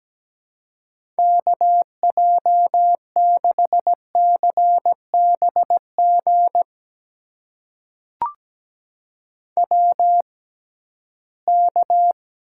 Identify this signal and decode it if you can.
Morse code, heard as a keyed tone, about 17 wpm, spaced as for 13 wpm, 700 Hz. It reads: KJ6CBG WK
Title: USA Call Signs - 6x Character Spacing - 17wpm